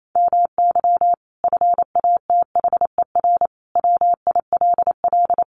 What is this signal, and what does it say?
Morse code, heard as a keyed tone, about 28 wpm, 700 Hz. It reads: MY FATHER WILL
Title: Sentences from Top 1100 Words 28wpm